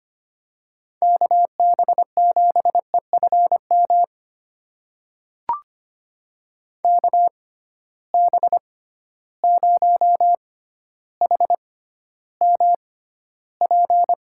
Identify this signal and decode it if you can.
Morse code, heard as a keyed tone, about 25 wpm, spaced as for 13 wpm, 700 Hz. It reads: KB7EFM KB0HMP
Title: USA Call Signs - 6 Characters - 6x Character Spacing - 25wpm